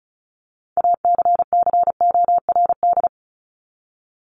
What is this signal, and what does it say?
Morse code, heard as a keyed tone, about 34 wpm, 700 Hz. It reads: ACCORD
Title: ING Suffix - Top 100 Words with Base Words 35wpm